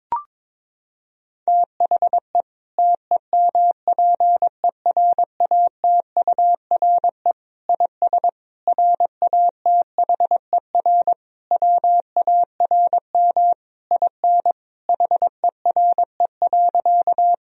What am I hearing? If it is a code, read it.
Morse code, heard as a keyed tone, about 22 wpm, 700 Hz. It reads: THE TEMPERATURE IS RATHER WARM IN HERE.
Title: Sentences from Top 1000 Words - Review Part 4 - 22wpm